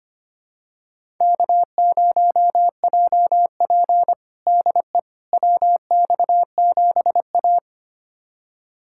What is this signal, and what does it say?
Morse code, heard as a keyed tone, about 25 wpm, 700 Hz. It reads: K0JP DE WX7A
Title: USA Callsign DE Callsign - 4 Characters - 25wpm